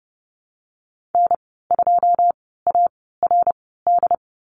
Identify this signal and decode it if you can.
Morse code, heard as a keyed tone, about 30 wpm, spaced as for 17 wpm, 700 Hz. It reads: N2ARD
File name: Lesson 14g - Number 2 - Call Signs 30wpm